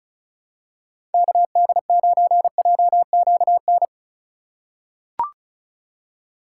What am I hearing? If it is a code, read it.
Morse code, heard as a keyed tone, about 36 wpm, 700 Hz. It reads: KD9JQN
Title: USA Call Signs - 4x Character Spacing - 35wpm